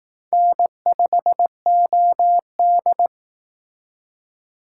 Morse code, heard as a keyed tone, about 18 wpm, 700 Hz. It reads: N5OD
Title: Speed Racing - USA Call Signs 15wpm